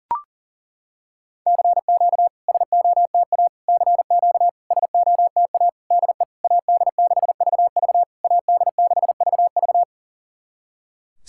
Morse code, heard as a keyed tone, about 37 wpm, 690 Hz. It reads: CQ SOTA CQ SOTA DE AD6VV AD6VV
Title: SOTA QSO Practice 40wpm